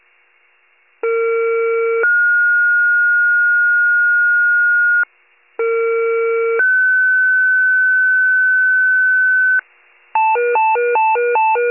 Two-tone page, heard as a one-second tone, then a three-second tone, about 460 then 1430 Hz, about 460 then 1500 Hz. It is heard from 1.0 to 9.6 s.
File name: wcalert